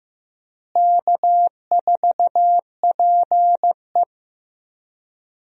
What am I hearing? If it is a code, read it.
Morse code, heard as a keyed tone, about 15 wpm, 700 Hz. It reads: K4PE
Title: USA Call Signs - 4 Characters - Repeated 3x - 15wpm